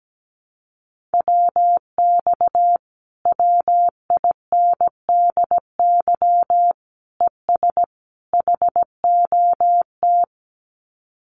Morse code, heard as a keyed tone, about 17 wpm, 700 Hz. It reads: WX WINDY ES HOT
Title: Common QSO Phrases Encore 17wpm